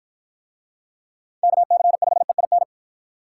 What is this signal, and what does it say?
Morse code, heard as a keyed tone, about 53 wpm, 700 Hz. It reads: KC5EIN